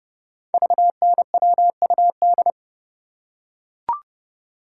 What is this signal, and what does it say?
Morse code, heard as a keyed tone, about 28 wpm, 700 Hz. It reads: VNWUD
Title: Code Groups - 5 Letters - 30wpm